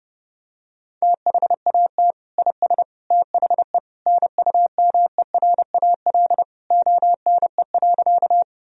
Morse code, heard as a keyed tone, about 32 wpm, 700 Hz. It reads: THAT IS THE NUMERAL ONE.
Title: Sentences from Top 1000 Words - Review Part 2 - 30wpm